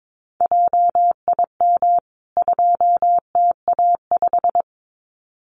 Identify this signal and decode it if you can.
Morse code, heard as a keyed tone, about 22 wpm, 700 Hz. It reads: JIM 2TA5
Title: CWT Exchange 22wpm